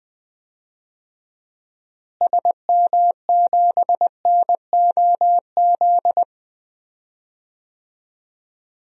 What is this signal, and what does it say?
Morse code, heard as a keyed tone, about 20 wpm, 700 Hz. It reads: SM7NOZ
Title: DXCC-100 Call Signs - 6 Characters - Repeated 3x - 20wpm